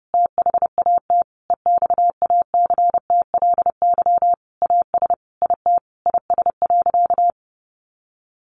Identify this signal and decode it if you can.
Morse code, heard as a keyed tone, about 28 wpm, 700 Hz. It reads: THAT EXACTLY AS IT IS.
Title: Sentences from Top 900 Words 30wpm